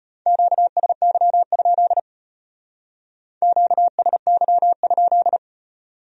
Morse code, heard as a keyed tone, about 30 wpm, 700 Hz. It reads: QSY? QSY?
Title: Speed-Racing Q-Codes - 25wpm